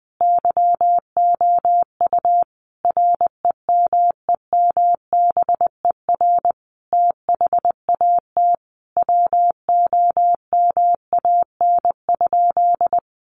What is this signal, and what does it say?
Morse code, heard as a keyed tone, about 20 wpm, 700 Hz. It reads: YOU REMEMBER THAT WOMAN?
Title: Sentences from Top 1100 Words 20wpm